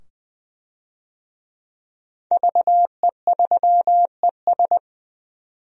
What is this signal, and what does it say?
Morse code, heard as a keyed tone, about 20 wpm, 700 Hz. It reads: VE3ES
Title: DXCC-100 Call Signs - 5 Characters - 20wpm